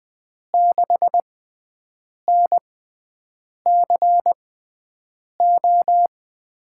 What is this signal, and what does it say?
Morse code, heard as a keyed tone, about 20 wpm, spaced as for 7 wpm, 700 Hz. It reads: 6NCO
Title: USA Call Signs - 5 Characters - 6x Character Spacing - 20wpm